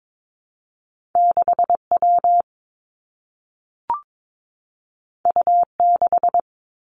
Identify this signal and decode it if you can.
Morse code, heard as a keyed tone, about 22 wpm, 700 Hz. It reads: 6 W U 6
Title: Two Challenging Letter-Number Combinations 22wpm